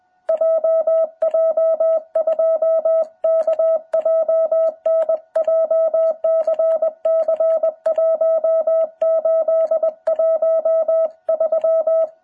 Morse code, heard as a keyed tone, about 20 wpm, 660 Hz. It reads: JJ2KJN JCC1813
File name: jh8cbh / 2021cw